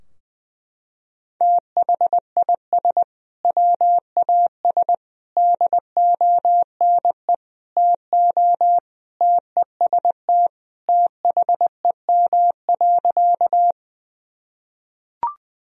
Morse code, heard as a keyed tone, about 21 wpm, 700 Hz. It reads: THIS WAS DONE TO TEST THEM.
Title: Sentences from Top 1000 Words - Review Part 3 - 20wpm